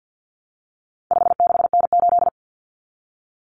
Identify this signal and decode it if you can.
Morse code, heard as a keyed tone, about 50 wpm, 700 Hz. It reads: 56N8
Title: Common Cut Numbers - 4-Digit 50wpm